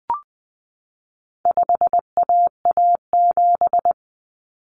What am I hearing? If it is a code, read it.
Morse code, heard as a keyed tone, about 20 wpm, 700 Hz. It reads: 5AA7
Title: Common Cut Numbers - 4-Digit 20wpm